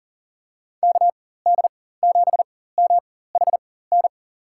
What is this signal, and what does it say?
Morse code, heard as a keyed tone, about 40 wpm, spaced as for 19 wpm, 700 Hz. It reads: KD7MHN
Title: USA Call Signs - 6 Characters - 4x Character Spacing - 40wpm